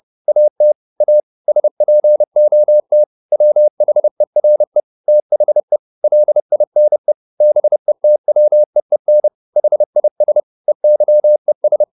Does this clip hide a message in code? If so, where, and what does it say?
Yes: Morse code from 0.3 to 11.8 s, heard as a keyed tone, about 30 wpm, 600 Hz. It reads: AT A SPOT WHERE THE LINE BETWEEN HIS EYES